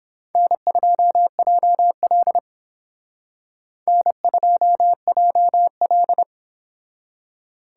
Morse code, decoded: N2JL N2JL

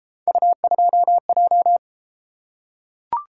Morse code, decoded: U2J

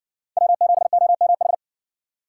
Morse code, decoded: W7OMH

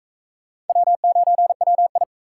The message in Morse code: W9WI

